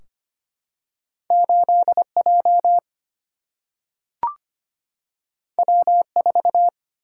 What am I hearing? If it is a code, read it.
Morse code, heard as a keyed tone, about 25 wpm, 700 Hz. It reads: 8 J W 4
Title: Single-Letter Single-Number 25wpm